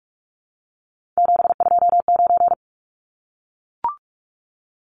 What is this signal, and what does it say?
Morse code, heard as a keyed tone, about 44 wpm, 700 Hz. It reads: 729
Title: Numbers - 3-Digit 45wpm